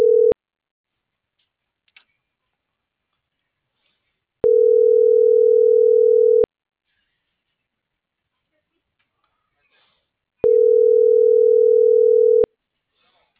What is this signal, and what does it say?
Ringback tone (North America), 440 + 480 Hz.